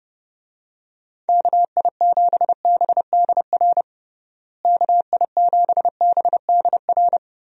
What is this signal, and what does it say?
Morse code, heard as a keyed tone, about 30 wpm, 700 Hz. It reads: KI7BDR KI7BDR
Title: USA Call Signs - 6 Characters - Repeated 3x - 30wpm